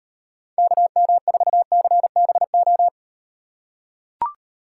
Morse code, heard as a keyed tone, about 36 wpm, 700 Hz. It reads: KM4CBO